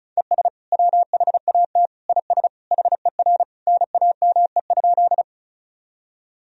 Morse code, heard as a keyed tone, about 36 wpm, 690 Hz. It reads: ES WHAT IS HER NAME?
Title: Sentences from Top 200 Words and Abbreviations 35wpm